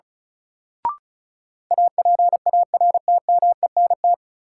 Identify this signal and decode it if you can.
Morse code, heard as a keyed tone, about 36 wpm, 700 Hz. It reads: APARTMENT
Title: Sentences from Top 900 Words - New Word Drill - 35wpm